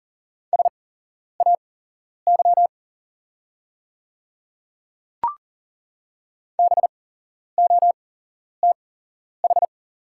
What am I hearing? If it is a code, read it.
Morse code, heard as a keyed tone, about 40 wpm, spaced as for 11 wpm, 690 Hz. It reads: SAY BOTH
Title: 8x Character Spacing - Top 500 Words - 40wpm